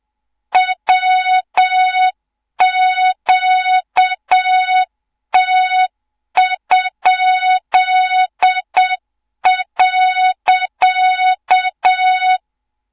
Morse code, heard as a keyed tone, about 7 wpm, 3700 Hz, with noise. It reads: WQT?.
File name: Morze-HA2MN / hang / 24 GYAKORLAT